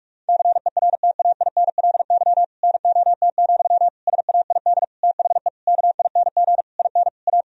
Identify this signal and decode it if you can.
Morse code, heard as a keyed tone, about 46 wpm, 700 Hz. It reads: CERTAINLY NOT, SAID THE KING IN A
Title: Sentences from Top 1000 Words - Review Part 1 - 45wpm